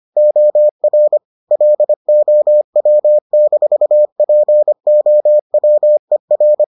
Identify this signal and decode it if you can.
Morse code, heard as a keyed tone, about 25 wpm, 600 Hz. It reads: OR LOW-POWER